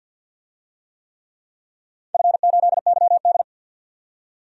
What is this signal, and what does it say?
Morse code, heard as a keyed tone, about 51 wpm, 700 Hz. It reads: F8YD
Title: DXCC-100 Call Signs - 4 Characters - 50wpm